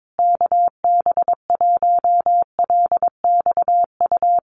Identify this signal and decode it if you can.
Morse code, heard as a keyed tone, about 22 wpm, 700 Hz. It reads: KB1LXU
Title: USA Call Signs Repeated 3x 22wpm